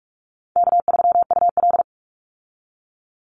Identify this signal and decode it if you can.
Morse code, heard as a keyed tone, about 44 wpm, 700 Hz. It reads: K3UL
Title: Easy Call Signs Repeated 2x - 45wpm